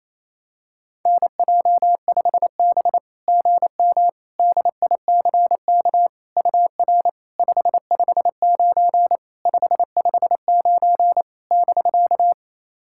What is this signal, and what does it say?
Morse code, heard as a keyed tone, about 28 wpm, 690 Hz. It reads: NJ5B GM DICK UR 559 559 <BK>
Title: SOTA QSO Practice 28wpm